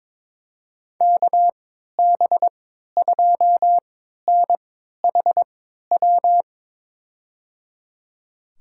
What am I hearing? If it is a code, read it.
Morse code, heard as a keyed tone, about 22 wpm, spaced as for 13 wpm, 700 Hz. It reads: KB2NHW
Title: USA Call Signs - 6 Characters - 3x Character Spacing - 22wpm